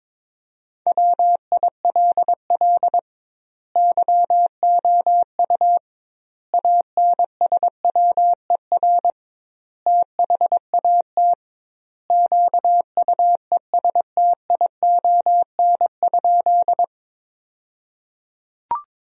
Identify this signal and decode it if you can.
Morse code, heard as a keyed tone, about 21 wpm, 700 Hz. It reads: WILL YOU ANSWER THAT QUESTION?
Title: 2x Word-Spacing - Sentences from Top 400 Words - 22wpm